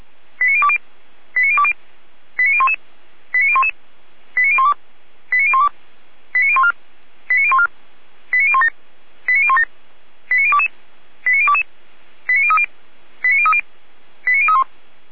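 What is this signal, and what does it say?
Five-tone selective call (ZVEI), heard as quick runs of five tones, tones of about 70 ms. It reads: 89020 89020 89011 89011 89012 89012 89014 89014 89017 89017 89022 89022 89030 89030 89031
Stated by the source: ZVEI five-tone selective call received on an SDR.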